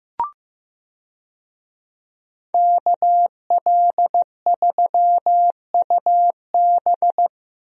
Morse code, heard as a keyed tone, about 15 wpm, 700 Hz. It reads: KL3UB
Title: USA Call Signs - 5 Characters - Repeated 2x - 15wpm